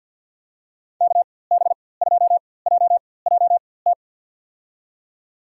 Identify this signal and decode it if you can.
Morse code, heard as a keyed tone, about 49 wpm, spaced as for 23 wpm, 700 Hz. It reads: KB2JJT
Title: USA Call Signs - 6 Characters - 4x Character Spacing - 50wpm